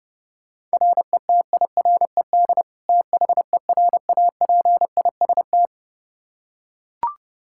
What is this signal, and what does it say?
Morse code, heard as a keyed tone, about 30 wpm, 700 Hz. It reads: RETIRED THERAPIST
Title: Practice QSO - Exchange 3 - More Info - 30wpm